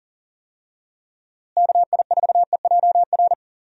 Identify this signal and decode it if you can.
Morse code, heard as a keyed tone, about 39 wpm, 700 Hz. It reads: KI4EJR
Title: USA Call Signs - 6 Characters - Repeated 3x - 40wpm